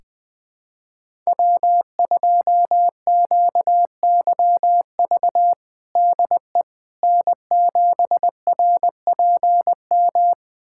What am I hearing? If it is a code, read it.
Morse code, heard as a keyed tone, about 20 wpm, 700 Hz. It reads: W2QYV DE N7RPM
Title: USA Callsign DE Callsign - 5 Characters - 20wpm